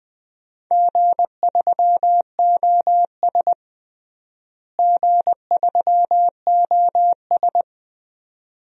Morse code, heard as a keyed tone, about 20 wpm, 700 Hz. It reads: G3OS G3OS
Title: DXCC-100 Call Signs - 4 Characters - Repeated 2x - 20wpm